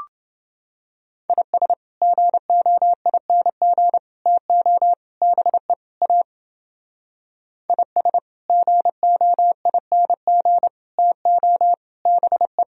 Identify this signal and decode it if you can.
Morse code, heard as a keyed tone, about 29 wpm, 700 Hz. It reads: IS GOING TO BE A IS GOING TO BE